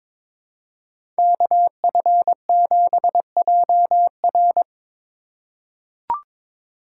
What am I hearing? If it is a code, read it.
Morse code, heard as a keyed tone, about 22 wpm, 700 Hz. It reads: KF7JR
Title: USA Call Signs - 5 Characters - 4x Character Spacing - 22wpm